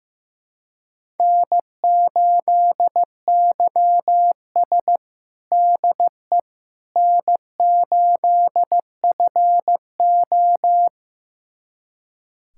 Morse code, heard as a keyed tone, about 15 wpm, 700 Hz. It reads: N8YS DE N8FO
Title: USA Callsign DE Callsign - 4 Characters - 15wpm